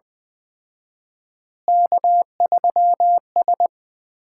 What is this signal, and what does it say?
Morse code, heard as a keyed tone, about 21 wpm, 700 Hz. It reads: K3S
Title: Common QSO Phrases - 4x Word-Spacing - 20wpm